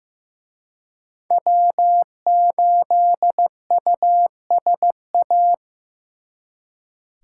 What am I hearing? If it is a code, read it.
Morse code, heard as a keyed tone, about 18 wpm, 700 Hz. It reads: W8USA
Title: Speed Racing - USA 1x3 Call Signs 15wpm